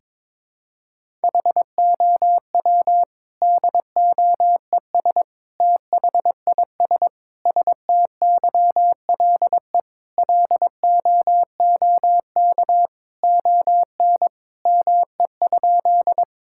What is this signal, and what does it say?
Morse code, heard as a keyed tone, about 21 wpm, 700 Hz. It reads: HOW DOES THIS STYLE LOOK ON ME?
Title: Sentences from Top 1000 Words - Review Part 5 - 22wpm